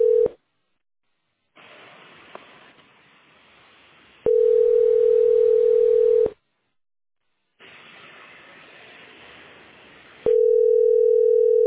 Ringback tone (North America), 440 + 480 Hz.